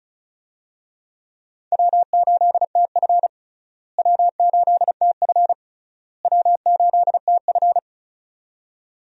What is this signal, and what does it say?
Morse code, heard as a keyed tone, about 36 wpm, 700 Hz. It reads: W8TF W8TF W8TF
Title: USA Call Signs - 4 Characters - Repeated 3x - 35wpm